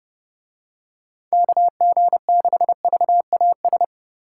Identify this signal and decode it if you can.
Morse code, heard as a keyed tone, about 29 wpm, 700 Hz. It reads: KG6VAS